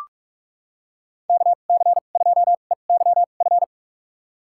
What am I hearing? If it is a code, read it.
Morse code, heard as a keyed tone, about 44 wpm, spaced as for 32 wpm, 700 Hz. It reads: KC2EYF
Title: USA Call Signs - 6 Characters - 2x Character Spacing - 45wpm